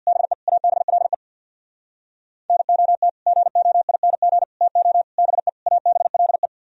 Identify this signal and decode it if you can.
Morse code, heard as a keyed tone, about 64 wpm, 700 Hz. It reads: BE ABLE NOT GOING TO BE ABLE